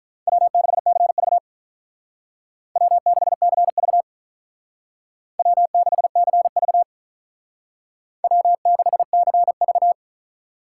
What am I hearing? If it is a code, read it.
Morse code, heard as a keyed tone, about 45 wpm, 700 Hz. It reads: W6CV W6CV W6CV W6CV